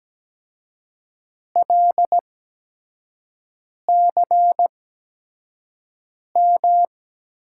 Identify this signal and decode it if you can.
Morse code, heard as a keyed tone, about 17 wpm, spaced as for 5 wpm, 700 Hz. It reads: LCM
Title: USA Call Signs - 6 Characters - 8x Character Spacing - 17wpm